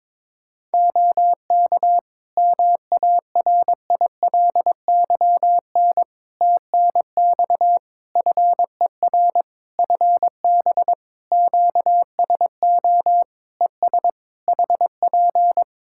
Morse code, heard as a keyed tone, about 22 wpm, 700 Hz. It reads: OK MARILYN TNX FER FB QSO ES HP
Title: Practice QSO - Exchange 4 - Ending - 22wpm